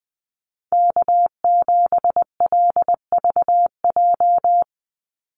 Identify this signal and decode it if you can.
Morse code, heard as a keyed tone, about 19 wpm, 700 Hz. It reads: K7LVJ
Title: Call Signs - 20wpm